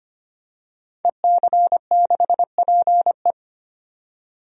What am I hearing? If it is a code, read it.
Morse code, heard as a keyed tone, about 25 wpm, 700 Hz. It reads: EC6PE
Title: DXCC-100 Call Signs - 5 Characters - 25wpm